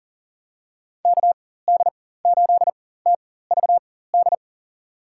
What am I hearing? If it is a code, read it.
Morse code, heard as a keyed tone, about 40 wpm, spaced as for 19 wpm, 700 Hz. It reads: KD8TVD
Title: USA Call Signs - 6 Characters - 4x Character Spacing - 40wpm